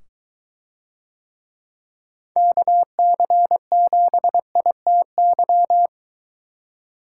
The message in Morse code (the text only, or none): KC7ITY